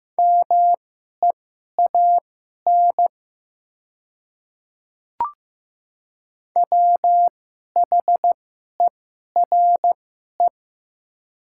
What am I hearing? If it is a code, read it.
Morse code, heard as a keyed tone, about 15 wpm, spaced as for 11 wpm, 700 Hz. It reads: MEAN WHERE
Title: 2x Character Spacing - Top 200 Words - 15wpm